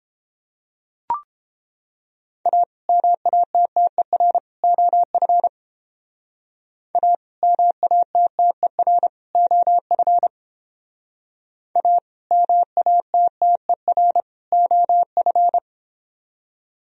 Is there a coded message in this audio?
Yes: Morse code from 2.5 to 15.6 s, heard as a keyed tone, about 29 wpm, 700 Hz. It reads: A MATTER OF A MATTER OF A MATTER OF